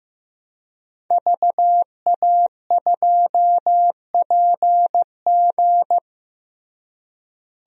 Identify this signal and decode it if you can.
Morse code, heard as a keyed tone, about 15 wpm, 700 Hz. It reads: VA2PG
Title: DXCC-100 Call Signs - 5 Characters - Repeated 3x - 15wpm